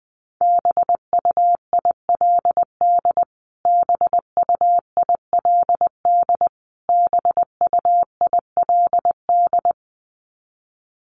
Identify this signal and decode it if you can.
Morse code, heard as a keyed tone, about 20 wpm, 700 Hz. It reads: BUILD BUILD BUILD
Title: Top 200 Words Repeated 20wpm